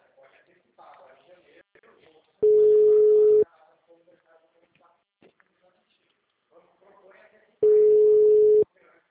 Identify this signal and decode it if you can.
Ringback tone (Europe, 425 Hz), with speech.